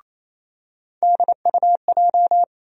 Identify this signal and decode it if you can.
Morse code, heard as a keyed tone, about 28 wpm, 700 Hz. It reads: DUJ
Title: USA Call Sign Postfixes - 3 Letters 28wpm